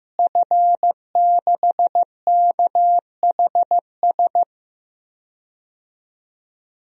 Morse code, heard as a keyed tone, about 16 wpm, 700 Hz. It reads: F6KHS
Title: DXCC-100 Call Signs - 5 Characters - Repeated 2x - 15wpm